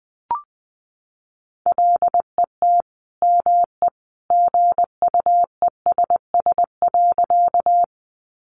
Morse code, heard as a keyed tone, about 20 wpm, 700 Hz. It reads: LET ME GUESS.
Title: Sentences from Top 1500 Words 20wpm